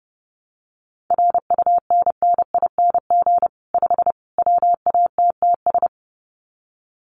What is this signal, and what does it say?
Morse code, heard as a keyed tone, about 28 wpm, 700 Hz. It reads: RUNNING 5 WATTS
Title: Common QSO Phrases 30wpm